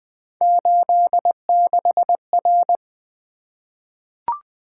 Morse code, heard as a keyed tone, about 20 wpm, 700 Hz. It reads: 86R
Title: Three Letter-Number Combinations - 20wpm